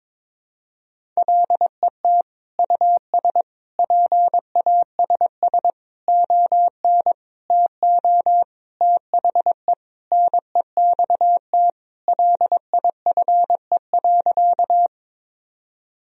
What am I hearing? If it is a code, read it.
Morse code, heard as a keyed tone, about 22 wpm, 700 Hz. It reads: LET US PASS ON TO THE NEXT LIFE.
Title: Sentences from Top 1000 Words - Review Part 1 - 22wpm